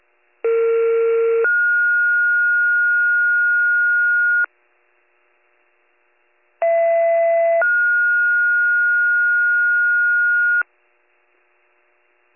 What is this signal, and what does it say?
Two-tone page, heard as a one-second tone, then a three-second tone, about 460 then 1430 Hz, about 670 then 1430 Hz. It is heard from 0.4 to 10.6 s.